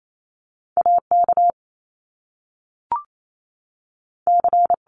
Morse code, heard as a keyed tone, about 28 wpm, 700 Hz. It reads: A K C
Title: Top Two Letter Combinations in Words 28wpm